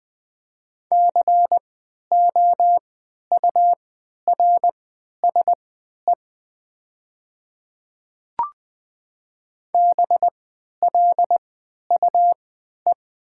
Morse code, heard as a keyed tone, about 20 wpm, spaced as for 11 wpm, 700 Hz. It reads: COURSE BLUE
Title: 3x Character Spacing - Top 500 Words - 20wpm